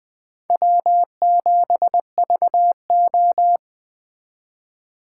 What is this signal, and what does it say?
Morse code, heard as a keyed tone, about 21 wpm, 700 Hz. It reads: W7VO